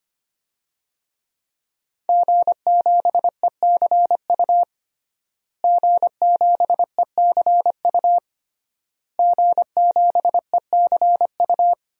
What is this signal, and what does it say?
Morse code, heard as a keyed tone, about 25 wpm, 700 Hz. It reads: G7ECU G7ECU G7ECU
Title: DXCC-100 Call Signs - 5 Characters - Repeated 3x - 25wpm